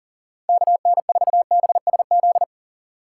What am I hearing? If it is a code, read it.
Morse code, heard as a keyed tone, about 40 wpm, 700 Hz. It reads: KN4BSZ
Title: USA Call Signs - 6 Characters - 6x Character Spacing - 40wpm